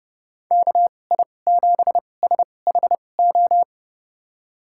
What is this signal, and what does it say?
Morse code, heard as a keyed tone, about 30 wpm, spaced as for 22 wpm, 700 Hz. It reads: KI7SHO
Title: USA Call Signs - 6 Characters - 2x Character Spacing - 30wpm